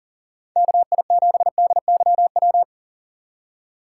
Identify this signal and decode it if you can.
Morse code, heard as a keyed tone, about 40 wpm, 700 Hz. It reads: KI7DYW
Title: USA Call Signs - 6 Characters - 2x Character Spacing - 40wpm